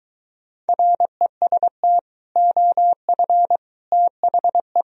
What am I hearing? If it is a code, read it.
Morse code, heard as a keyed tone, about 23 wpm, 700 Hz. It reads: REST OF THE